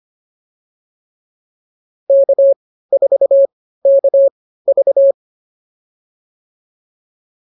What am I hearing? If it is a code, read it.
Morse code, heard as a keyed tone, about 25 wpm, spaced as for 15 wpm, 550 Hz. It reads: K4KV